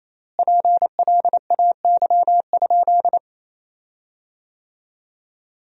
Morse code, heard as a keyed tone, about 28 wpm, 700 Hz. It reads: PLAY?
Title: 3x Word-Spacing - Sentences from Top 200 Words - 28wpm